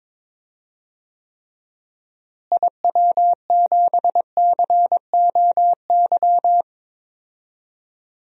Morse code, heard as a keyed tone, about 22 wpm, 700 Hz. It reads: IW7COY
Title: DXCC-100 Call Signs - 6 Characters - Repeated 2x - 22wpm